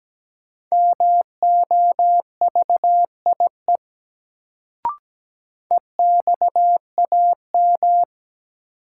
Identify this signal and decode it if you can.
Morse code, heard as a keyed tone, about 17 wpm, 700 Hz. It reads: MOVIE EXAM
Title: Sentences from Top 1000 Words - Review Part 5 - New Word Drill - 17wpm